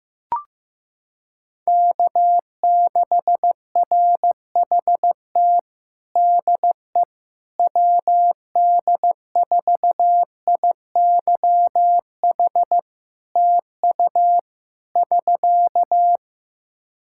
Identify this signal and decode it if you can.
Morse code, heard as a keyed tone, about 15 wpm, 700 Hz. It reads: K6RHT DE WD4IYH TU <SK>
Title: Practice QSO - Exchange 4 - Ending - 15wpm